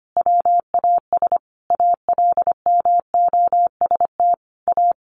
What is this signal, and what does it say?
Morse code, heard as a keyed tone, about 25 wpm, 700 Hz. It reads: WAS ALMOST A
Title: Sentences from Top 1200 Words 25wpm